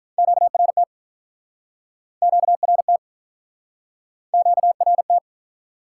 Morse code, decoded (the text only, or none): QRT QRT QRT